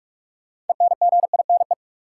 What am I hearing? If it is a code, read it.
Morse code, heard as a keyed tone, about 45 wpm, 700 Hz. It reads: ENGINE
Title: Sentences from Top 1500 Words - New Word Drill - 45wpm